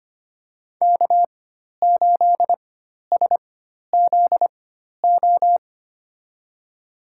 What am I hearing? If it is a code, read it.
Morse code, heard as a keyed tone, about 25 wpm, spaced as for 12 wpm, 700 Hz. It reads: K8SZO
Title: USA Call Signs - 5 Characters - 4x Character Spacing - 25wpm